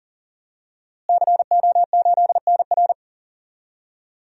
Morse code, heard as a keyed tone, about 40 wpm, 700 Hz. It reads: CO8NR
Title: DXCC-100 Call Signs - 5 Characters - Repeated 3x - 40wpm